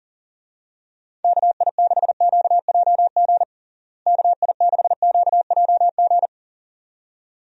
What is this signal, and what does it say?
Morse code, heard as a keyed tone, about 40 wpm, 700 Hz. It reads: KI6QJG KI6QJG
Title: USA Call Signs - 6 Characters - Repeated 2x - 40wpm